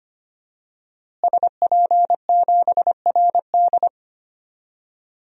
Morse code, heard as a keyed tone, about 25 wpm, 700 Hz. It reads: SP7RD
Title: DXCC-100 Call Signs - 5 Characters - Repeated 2x - 25wpm